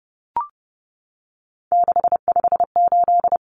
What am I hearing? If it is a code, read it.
Morse code, heard as a keyed tone, about 30 wpm, 700 Hz. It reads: 658
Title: Numbers - 3-Digit 30wpm